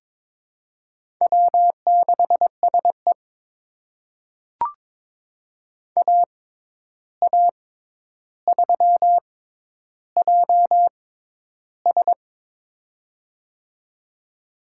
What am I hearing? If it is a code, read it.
Morse code, heard as a keyed tone, about 22 wpm, spaced as for 11 wpm, 700 Hz. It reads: W6SE AA3JS
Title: USA Call Signs - 6x Character Spacing - 22wpm